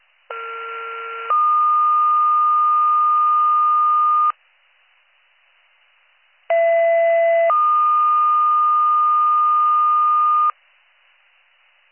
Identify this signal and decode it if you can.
Two-tone page, heard as a one-second tone, then a three-second tone, about 1370 then 1220 Hz, about 670 then 1220 Hz. It is heard from 0.3 to 10.5 s.